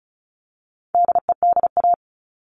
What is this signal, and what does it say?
Morse code, heard as a keyed tone, about 34 wpm, 700 Hz. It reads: DEDA
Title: All Cut Numbers - 4-Digit 35wpm